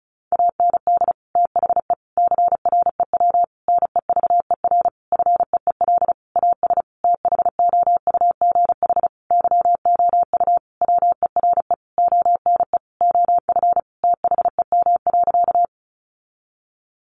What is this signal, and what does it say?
Morse code, heard as a keyed tone, about 34 wpm, 700 Hz. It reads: AND THE CREW NEVER FEEL AS THOUGH YOU WERE ONE OF THEM.
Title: Sentences from Top 1500 Words 35wpm